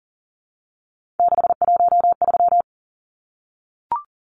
Morse code, heard as a keyed tone, about 40 wpm, 700 Hz. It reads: 613
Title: Numbers - 3-Digit 40wpm